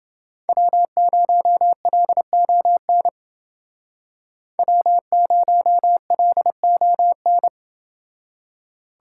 Morse code, heard as a keyed tone, about 28 wpm, 700 Hz. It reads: W0LON W0LON